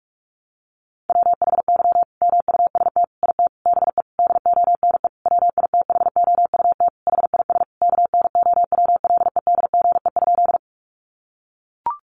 Morse code, heard as a keyed tone, about 44 wpm, 700 Hz. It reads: WHY MUST IT BE DONE WITHOUT HIS KNOWLEDGE?
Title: Sentences from Top 600 Words 45wpm